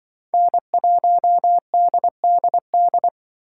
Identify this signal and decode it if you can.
Morse code, heard as a keyed tone, about 24 wpm, 700 Hz. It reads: N1DDD
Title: Speed Racing - USA 1x3 Call Signs 20wpm